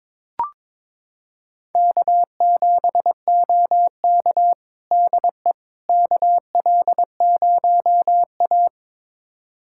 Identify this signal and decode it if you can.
Morse code, heard as a keyed tone, about 22 wpm, 700 Hz. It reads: K7OK DE KL0A
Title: USA Callsign DE Callsign - 4 Characters - 22wpm